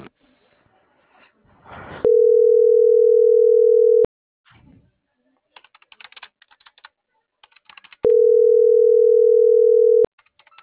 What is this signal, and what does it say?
Ringback tone (North America), 440 + 480 Hz.